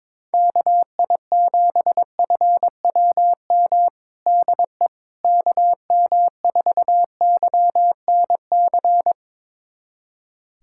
Morse code, heard as a keyed tone, about 21 wpm, 700 Hz. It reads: KI7FWM DE KM4YNC
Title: USA Callsign DE Callsign - 6 Characters - 22wpm